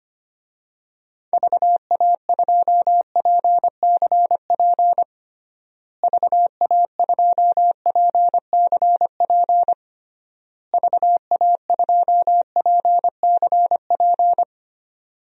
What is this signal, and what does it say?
Morse code, heard as a keyed tone, about 25 wpm, 700 Hz. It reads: VA2PCP VA2PCP VA2PCP
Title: DXCC-100 Call Signs - 6 Characters - Repeated 3x - 25wpm